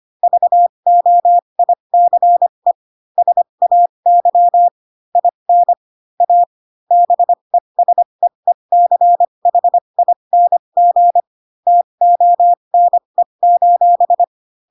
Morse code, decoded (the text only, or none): VOICE SAY IN A BESEECHING TONE: